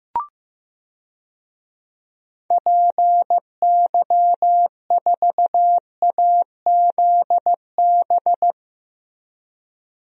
Morse code, heard as a keyed tone, about 15 wpm, 700 Hz. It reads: PY4AZB
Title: DXCC-100 Call Signs - 6 Characters - Repeated 3x - 15wpm